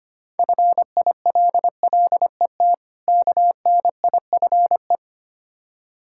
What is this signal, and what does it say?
Morse code, heard as a keyed tone, about 26 wpm, 700 Hz. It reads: FILLET KNIFE
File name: practice / 25wpm / Difficult Words - Speed-Racing with Hints - 25wpm